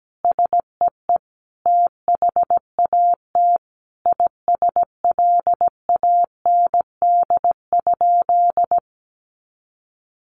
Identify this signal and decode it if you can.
Morse code, heard as a keyed tone, about 17 wpm, 700 Hz. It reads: SEE THAT ISLAND?
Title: Sentences from Top 500 Words 17wpm